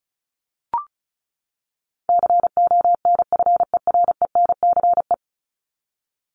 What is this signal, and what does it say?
Morse code, heard as a keyed tone, about 34 wpm, 700 Hz. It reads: CONFERENCE
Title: Top 1751-2000 Words 35wpm